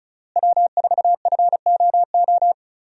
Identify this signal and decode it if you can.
Morse code, heard as a keyed tone, about 36 wpm, 700 Hz. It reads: W4FOO
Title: USA Call Signs - 5 Characters - Repeated 3x - 35wpm